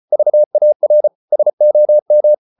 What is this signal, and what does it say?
Morse code, heard as a keyed tone, about 34 wpm, 600 Hz. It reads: VAR SOM
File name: Ferie QRQ 34wpm